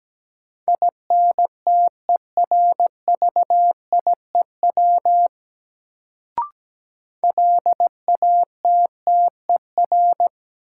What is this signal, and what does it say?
Morse code, heard as a keyed tone, about 17 wpm, 700 Hz. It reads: INTERVIEW LATTER
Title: Sentences from Top 1600 Words - New Word Drill Alt - 17wpm